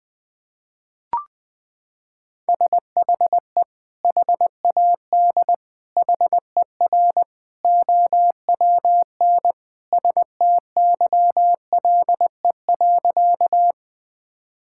Morse code, heard as a keyed tone, about 20 wpm, 700 Hz. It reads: SHE HAD HER OWN STYLE.
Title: Sentences from Top 1000 Words - Review Part 5 - 20wpm